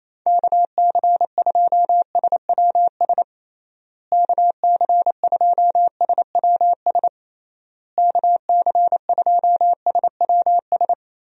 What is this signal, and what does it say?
Morse code, heard as a keyed tone, about 28 wpm, 700 Hz. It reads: KC2SWS KC2SWS KC2SWS
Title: USA Call Signs - 6 Characters - Repeated 3x - 28wpm